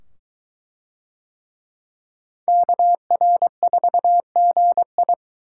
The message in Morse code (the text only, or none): KR4GI